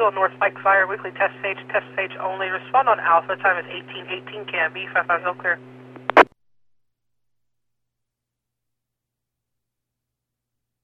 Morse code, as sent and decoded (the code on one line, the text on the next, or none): none